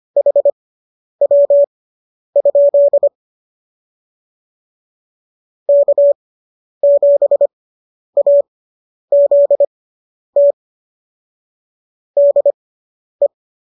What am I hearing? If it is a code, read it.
Morse code, heard as a keyed tone, about 25 wpm, spaced as for 10 wpm, 570 Hz. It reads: HW? K7AZT DE